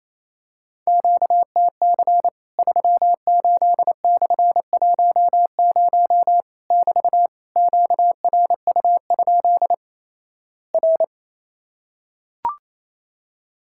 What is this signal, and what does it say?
Morse code, heard as a keyed tone, about 28 wpm, 700 Hz. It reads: QTC 38/10 <BT> QRU? R
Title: Worked All Europe DX Contest - QTC Exchange - 28wpm